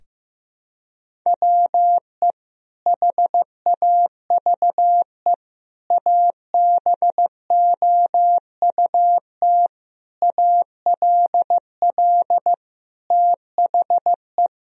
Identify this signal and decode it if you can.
Morse code, heard as a keyed tone, about 16 wpm, 700 Hz. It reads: WE HAVE ABOUT ALL THE
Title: Sentences from Top 1000 Words - Review Part 1 - 15wpm